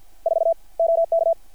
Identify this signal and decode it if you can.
Morse code, heard as a keyed tone, about 50 wpm, 660 Hz. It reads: 4 OK
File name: archived / CRSnet